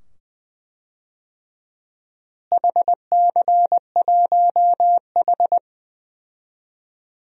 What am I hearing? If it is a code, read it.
Morse code, heard as a keyed tone, about 20 wpm, 700 Hz. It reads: HC1H
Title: DXCC-100 Call Signs - 4 Characters - Repeated 2x - 20wpm